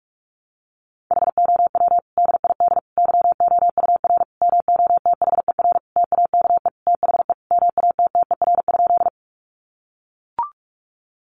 Morse code, heard as a keyed tone, about 44 wpm, 700 Hz. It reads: HOW DID YOUR MOTHER TAKE THE MATTER?
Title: Sentences from Top 700 Words 45wpm